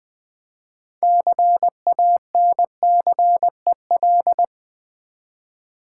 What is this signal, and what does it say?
Morse code, heard as a keyed tone, about 20 wpm, 700 Hz. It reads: CANCEL
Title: Build A Word - 2 Letters At A Time - 6-Letter Words - 20wpm